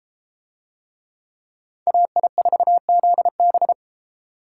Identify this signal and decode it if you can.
Morse code, heard as a keyed tone, about 31 wpm, 700 Hz. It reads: AI4ZB